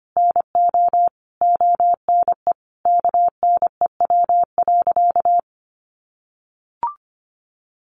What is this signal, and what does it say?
Morse code, heard as a keyed tone, about 25 wpm, 700 Hz. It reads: NO ONE KNEW.
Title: Sentences from Top 400 Words 25wpm